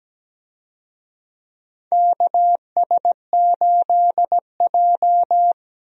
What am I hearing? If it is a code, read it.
Morse code, heard as a keyed tone, about 17 wpm, 700 Hz. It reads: KS8J
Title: USA Call Signs - 4 Characters - Repeated 3x - 17wpm